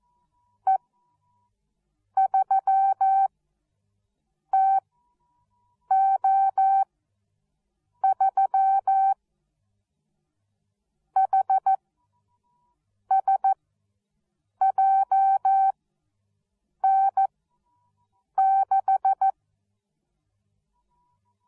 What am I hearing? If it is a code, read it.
Morse code, heard as a keyed tone, about 14 wpm, spaced as for 6 wpm, 770 Hz. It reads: E3TO3 HSJN6